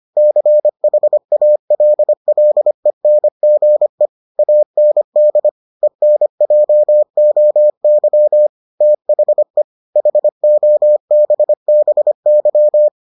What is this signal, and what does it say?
Morse code, heard as a keyed tone, about 25 wpm, 600 Hz. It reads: CHALLENGE AND ENJOY THE HOBBY